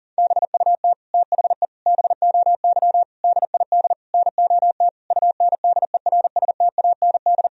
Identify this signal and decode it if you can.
Morse code, heard as a keyed tone, about 40 wpm, 700 Hz. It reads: BUT THE BOY DID NOT UNDERSTAND
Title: Sentences from Top 1000 Words - Review Part 3 - 40wpm